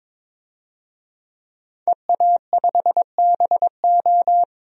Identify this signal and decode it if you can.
Morse code, heard as a keyed tone, about 21 wpm, 700 Hz. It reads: EA5BO
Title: DXCC-100 Call Signs - 5 Characters - 22wpm